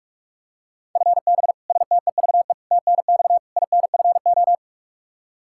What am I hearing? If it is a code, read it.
Morse code, heard as a keyed tone, about 41 wpm, 700 Hz. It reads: FB STEVE TNX INFO
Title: Practice QSO - Exchange 3 - More Info - 45wpm